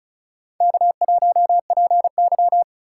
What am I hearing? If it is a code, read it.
Morse code, heard as a keyed tone, about 36 wpm, 700 Hz. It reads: K1PY
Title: USA Call Signs - 4 Characters - 3x Character Spacing - 35wpm